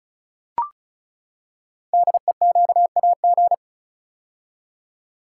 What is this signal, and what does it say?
Morse code, heard as a keyed tone, about 35 wpm, 700 Hz. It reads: DEQAG
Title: Code Groups - 5 Letters - 35wpm